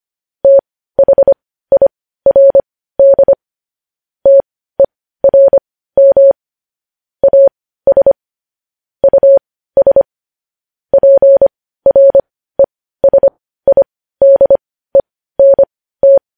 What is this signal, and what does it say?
Morse code, heard as a keyed tone, about 25 wpm, spaced as for 15 wpm, 550 Hz. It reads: THIRD TERM AS US PRESIDENT
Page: News Headlines in Morse Code at 15 WPM